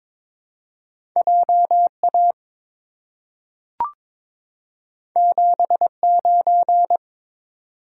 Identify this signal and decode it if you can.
Morse code, heard as a keyed tone, about 23 wpm, 700 Hz. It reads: JA 79
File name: practice / 22wpm / Two Letter-Number Combinations - 22wpm